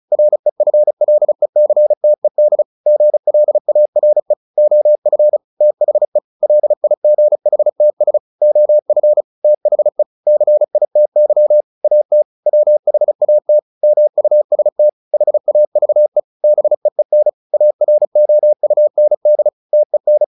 Morse code, heard as a keyed tone, about 34 wpm, 600 Hz. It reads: REFLECTED GLARE OF THE LIGHTS OF THE CITY AT WHAT MUST HAVE BEEN AROUND TEN
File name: the old man and the sea 2 35wpm